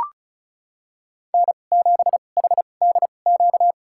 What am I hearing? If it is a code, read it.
Morse code, heard as a keyed tone, about 34 wpm, spaced as for 25 wpm, 700 Hz. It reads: N7HDQ
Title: USA Call Signs - 2x Character Spacing - 35wpm